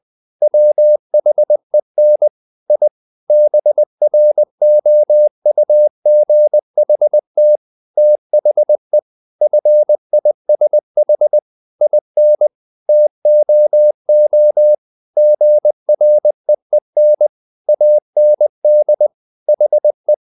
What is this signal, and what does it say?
Morse code, heard as a keyed tone, about 20 wpm, 600 Hz. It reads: WHEN I BROUGHT THE FISH IN TOO GREEN AND HE
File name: the old man and the sea / the old man and the sea part 1 20wpm